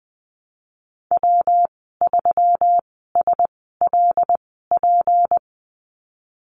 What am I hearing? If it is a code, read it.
Morse code, heard as a keyed tone, about 20 wpm, spaced as for 15 wpm, 700 Hz. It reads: W3SLP
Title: 173 Lesson 28g - Number 9 - Call Signs 20wpm